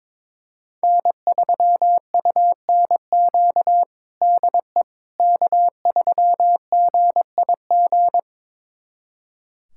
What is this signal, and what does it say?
Morse code, heard as a keyed tone, about 22 wpm, 700 Hz. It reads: N3UNQ DE K3GIG
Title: USA Callsign DE Callsign - 5 Characters - 22wpm